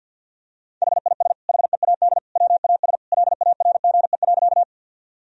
Morse code, heard as a keyed tone, about 51 wpm, 700 Hz. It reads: HIS HEAD WAS LARGE.
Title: Sentences from Top 1000 Words - Review Part 1 - 50wpm